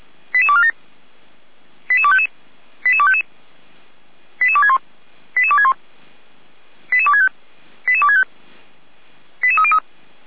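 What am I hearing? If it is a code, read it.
Five-tone selective call (ZVEI), heard as quick runs of five tones, tones of about 70 ms. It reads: 88247 88255 88255 88261 88261 88265 88265 88303
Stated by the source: ZVEI five-tone selective call received with an SDR.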